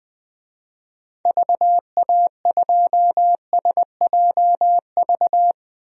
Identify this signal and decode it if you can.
Morse code, heard as a keyed tone, about 20 wpm, 700 Hz. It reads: VA2SJV
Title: DXCC-100 Call Signs - 6 Characters - Repeated 2x - 20wpm